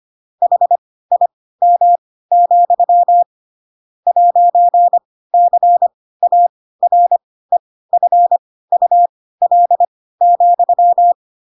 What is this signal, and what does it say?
Morse code, heard as a keyed tone, about 25 wpm, spaced as for 16 wpm, 700 Hz. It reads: HIM, 'CAREFUL,
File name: Pinocchio / Pinocchio ch7 10 25 ik1hge